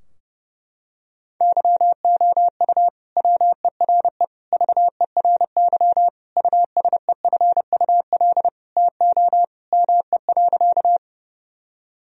Morse code, heard as a keyed tone, about 30 wpm, 700 Hz. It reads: YOU WERE VERY USEFUL TO ME.
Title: Sentences from Top 1000 Words - Review Part 4 - 30wpm